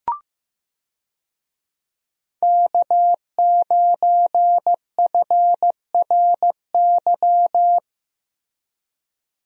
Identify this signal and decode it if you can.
Morse code, heard as a keyed tone, about 15 wpm, 700 Hz. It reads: K9FRY
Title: USA Call Signs - 5 Characters - Repeated 3x - 15wpm